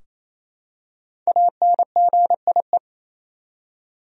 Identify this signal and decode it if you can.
Morse code, heard as a keyed tone, about 28 wpm, 700 Hz. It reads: ANGIE
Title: USA Common Names - Max 5 Letters - 28wpm